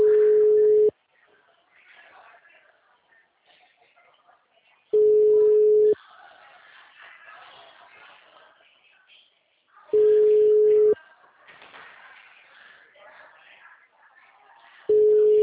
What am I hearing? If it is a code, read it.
Ringback tone (Europe, 425 Hz).